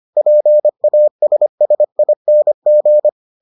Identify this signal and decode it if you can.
Morse code, heard as a keyed tone, about 25 wpm, 600 Hz. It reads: PASSING